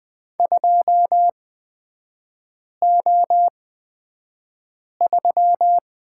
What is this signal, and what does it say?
Morse code, heard as a keyed tone, about 20 wpm, spaced as for 5 wpm, 700 Hz. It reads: 2O3
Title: 255 Lesson 41e - Period - Three Characters 20wpm